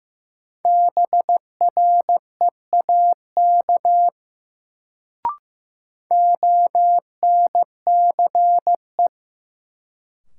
Morse code, heard as a keyed tone, about 15 wpm, 700 Hz. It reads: BREAK ONCE
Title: Sets of 4 Words - New Word Drill - 15wpm